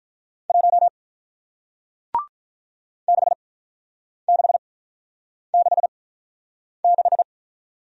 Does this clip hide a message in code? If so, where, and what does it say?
Yes: Morse code from 0.5 to 7.2 s, heard as a keyed tone, about 48 wpm, spaced as for 9 wpm, 700 Hz. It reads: 1 6666